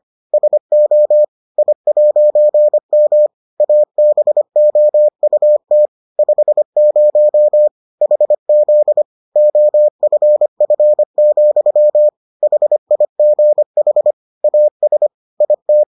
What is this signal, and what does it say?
Morse code, heard as a keyed tone, about 25 wpm, 600 Hz. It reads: SO I'M ABOUT 50 HZ OFF, HIGH AS IT